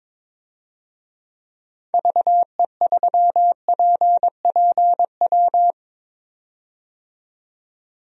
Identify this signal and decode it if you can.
Morse code, heard as a keyed tone, about 22 wpm, 700 Hz. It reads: VE3PPW
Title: DXCC-100 Call Signs - 6 Characters - 22wpm